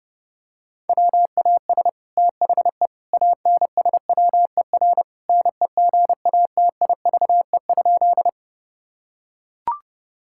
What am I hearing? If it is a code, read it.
Morse code, heard as a keyed tone, about 30 wpm, 700 Hz. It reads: WAS THE ANSWER NEGATIVE?
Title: Sentences from Top 1000 Words - Review Part 6 - 30wpm